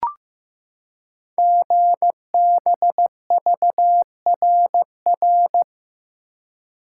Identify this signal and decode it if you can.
Morse code, heard as a keyed tone, about 15 wpm, 700 Hz. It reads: GBVRR
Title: Code Groups - 5 Letters - 15wpm